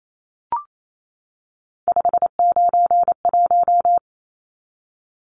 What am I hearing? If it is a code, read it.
Morse code, heard as a keyed tone, about 28 wpm, 700 Hz. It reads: 591
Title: Numbers - 3-Digit 28wpm